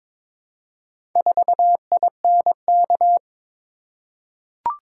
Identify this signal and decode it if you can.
Morse code, heard as a keyed tone, about 22 wpm, 700 Hz. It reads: 4INK
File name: practice / 22wpm / Four Letter-Number Combinations - 22wpm